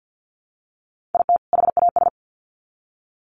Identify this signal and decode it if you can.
Morse code, heard as a keyed tone, about 50 wpm, 700 Hz. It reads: IT HAS
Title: Sets of 2 Words 50wpm